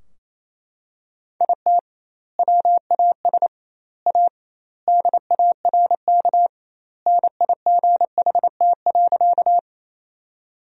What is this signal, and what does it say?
Morse code, heard as a keyed tone, about 28 wpm, 700 Hz. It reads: IT WAS A DARK NIGHT.
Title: 2x Word-Spacing - Sentences from Top 500 Words - 28wpm